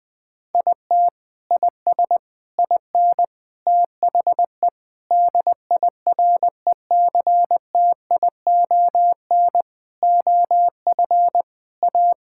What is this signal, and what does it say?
Morse code, heard as a keyed tone, about 20 wpm, 700 Hz. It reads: IT IS IN THE DIRECTION OF A
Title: Sentences from Top 1000 Words - Review Part 5 - 20wpm